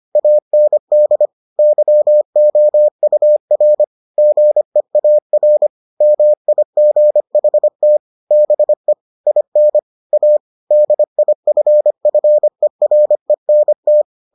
Morse code, heard as a keyed tone, about 25 wpm, 600 Hz. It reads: AND YOUR GEAR MIGHT BE IN A DIFFERENT